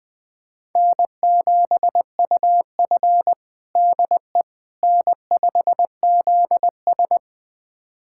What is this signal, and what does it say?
Morse code, heard as a keyed tone, about 20 wpm, 700 Hz. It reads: N7UF DE N5ZS
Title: USA Callsign DE Callsign - 4 Characters - 20wpm